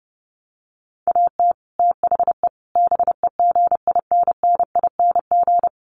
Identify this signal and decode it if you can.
Morse code, heard as a keyed tone, about 30 wpm, 700 Hz. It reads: AT THE BEGINNING
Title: Sets of 5 Words 30wpm